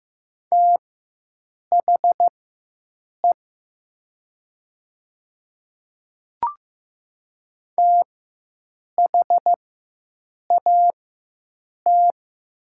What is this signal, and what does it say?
Morse code, heard as a keyed tone, about 15 wpm, spaced as for 7 wpm, 700 Hz. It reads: THE THAT
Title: 4x Character Spacing - Top 100 Words - 15wpm